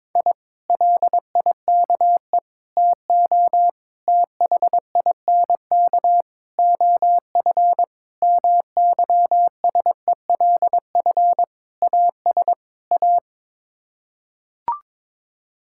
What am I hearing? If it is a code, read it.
Morse code, heard as a keyed tone, about 22 wpm, 700 Hz. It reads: I LIKE TO THINK OF MYSELF AS A
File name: practice / 22wpm / Sets of 8 Words - 22wpm